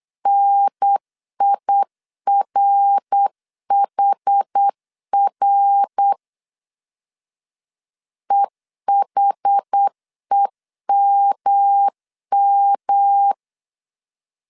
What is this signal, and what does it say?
Morse code, heard as a keyed tone, about 8 wpm, 800 Hz. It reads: NIRHR EHEMM
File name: CW / DET / 1 3 1500C S B 8wpm 4141